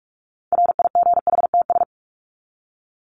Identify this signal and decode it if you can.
Morse code, heard as a keyed tone, about 44 wpm, 700 Hz. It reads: RIGHTS
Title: ES:S Suffix - Top 100 Words with Base Words 45wpm